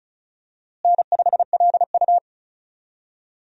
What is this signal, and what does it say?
Morse code, heard as a keyed tone, about 36 wpm, 700 Hz. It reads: N5LU
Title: USA Call Signs - 4 Characters - Repeated 3x - 35wpm